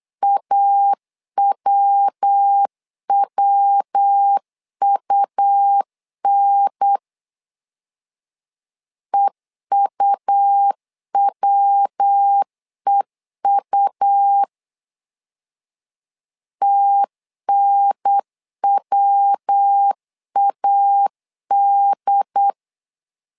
Morse code, heard as a keyed tone, about 8 wpm, 800 Hz. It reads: AWWUN EUWEU TNWAD
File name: LEZIONI / 3 MWRO / 3 MW 8wpm 2501